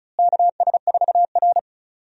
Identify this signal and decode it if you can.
Morse code, heard as a keyed tone, about 36 wpm, 700 Hz. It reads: KS4R
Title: USA Call Signs - 4 Characters - Repeated 3x - 35wpm